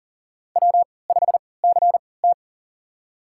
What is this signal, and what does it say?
Morse code, heard as a keyed tone, about 40 wpm, spaced as for 23 wpm, 700 Hz. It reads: W5CT
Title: USA Call Signs - 4 Characters - 3x Character Spacing - 40wpm